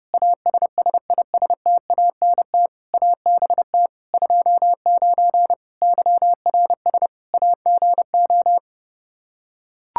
Morse code, heard as a keyed tone, about 31 wpm, 700 Hz. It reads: ASSISTANT ABT 29 YRS AGO
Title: Practice QSO - Exchange 3 - More Info - 30wpm